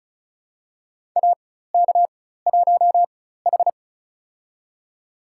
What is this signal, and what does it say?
Morse code, heard as a keyed tone, about 34 wpm, spaced as for 16 wpm, 700 Hz. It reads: AK1H
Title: USA Call Signs - 4x Character Spacing - 35wpm